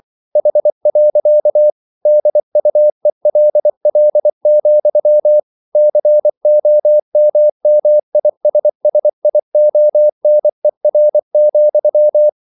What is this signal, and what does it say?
Morse code, heard as a keyed tone, about 24 wpm, 600 Hz. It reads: H. DUELL, COMMISSIONER,
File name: CW / Practice / EnglishQuotations0424